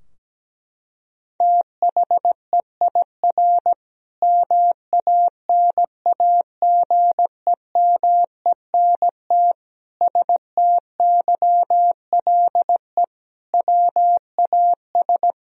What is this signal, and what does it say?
Morse code, heard as a keyed tone, about 17 wpm, 700 Hz. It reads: THEIR MANAGEMENT STYLE WAS
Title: Sentences from Top 1000 Words - Review Part 4 - 17wpm